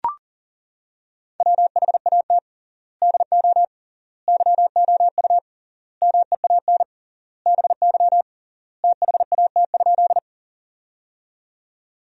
Morse code, heard as a keyed tone, about 37 wpm, 700 Hz. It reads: WHAT DO YOU MEAN BY THAT?
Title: 3x Word-Spacing - Sentences from Top 200 Words - 40wpm